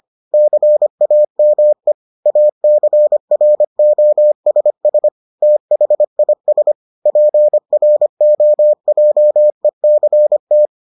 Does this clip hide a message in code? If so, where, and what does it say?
Yes: Morse code from 0.3 to 10.7 s, heard as a keyed tone, about 25 wpm, 600 Hz. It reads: CAME ACROSS THIS PROJECT